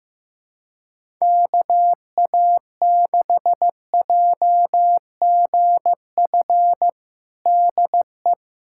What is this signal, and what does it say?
Morse code, heard as a keyed tone, about 15 wpm, 700 Hz. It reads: KA6JGF DE
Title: USA Callsign DE Callsign - 6 Characters - 15wpm